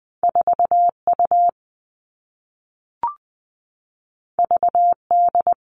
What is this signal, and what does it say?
Morse code, heard as a keyed tone, about 20 wpm, 700 Hz. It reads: 4U VD
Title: All Cut Numbers - 2-Digit 20wpm